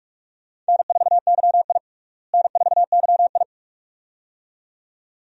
Morse code, heard as a keyed tone, about 46 wpm, 700 Hz. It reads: N4YI N4YI
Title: USA Call Signs - 4 Characters - Repeated 2x - 45wpm